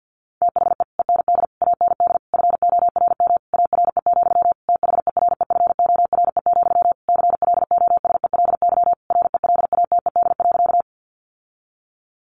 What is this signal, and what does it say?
Morse code, heard as a keyed tone, about 50 wpm, 700 Hz. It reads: THE END AND FORM ARE, THEREFORE, CLOSELY RELATED.
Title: Sentences from Top 1400 Words 50wpm